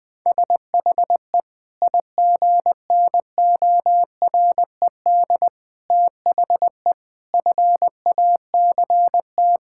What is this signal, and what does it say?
Morse code, heard as a keyed tone, about 20 wpm, 700 Hz. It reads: SHE IGNORED THE FACT
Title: Sentences with 5-Word Phrases - 20wpm